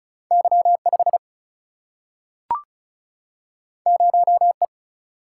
Morse code, heard as a keyed tone, about 36 wpm, 700 Hz. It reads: Y5 0E
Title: Single-Letter Single-Number 35wpm